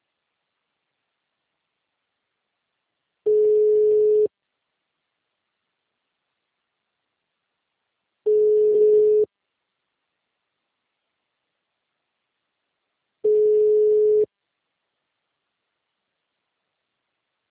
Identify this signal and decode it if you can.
Ringback tone (Europe, 425 Hz).
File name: out- -2001- - - .76